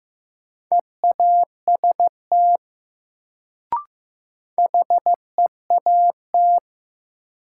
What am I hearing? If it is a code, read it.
Morse code, heard as a keyed tone, about 15 wpm, 700 Hz. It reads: EAST HEAT
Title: Sentences from Top 1000 Words - Review Part 3 - New Word Drill - 15wpm